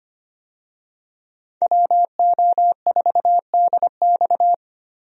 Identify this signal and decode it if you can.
Morse code, heard as a keyed tone, about 25 wpm, 700 Hz. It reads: WO4DX
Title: USA Call Signs - 5 Characters - Repeated 2x - 25wpm